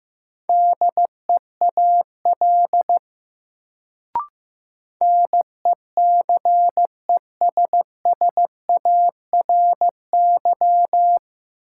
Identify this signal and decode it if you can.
Morse code, heard as a keyed tone, about 15 wpm, 700 Hz. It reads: DEAL NECESSARY
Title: Sentences from Top 800 Words - New Word Drill - 15wpm